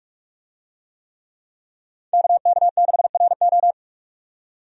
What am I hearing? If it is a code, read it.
Morse code, heard as a keyed tone, about 45 wpm, 700 Hz. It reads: KK6RO
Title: USA Call Signs - 5 Characters - 45wpm